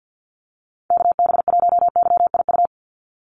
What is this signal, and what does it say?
Morse code, heard as a keyed tone, about 50 wpm, 700 Hz. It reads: KB1YIU
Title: USA Call Signs 50wpm